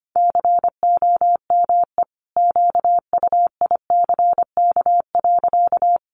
Morse code, heard as a keyed tone, about 25 wpm, 700 Hz. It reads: COME QUICK.
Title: Sentences from Top 500 Words 25wpm